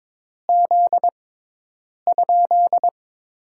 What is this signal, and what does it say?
Morse code, heard as a keyed tone, about 21 wpm, 700 Hz. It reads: Z ?
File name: Top 90 QSO Elements - 6x Character Spacing - 22wpm